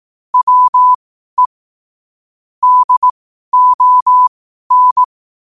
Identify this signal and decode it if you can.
Morse code, heard as a keyed tone, about 18 wpm, spaced as for 12 wpm, 1000 Hz. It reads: WE DON
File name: qotdfiles / 12wpm